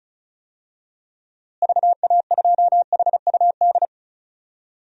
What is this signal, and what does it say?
Morse code, heard as a keyed tone, about 36 wpm, 700 Hz. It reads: VA2HUD
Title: DXCC-100 Call Signs - 6 Characters - Repeated 2x - 35wpm